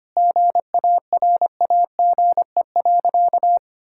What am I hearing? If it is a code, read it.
Morse code, heard as a keyed tone, about 25 wpm, 700 Hz. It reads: GARAGE.
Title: Sentences from Top 1600 Words 25wpm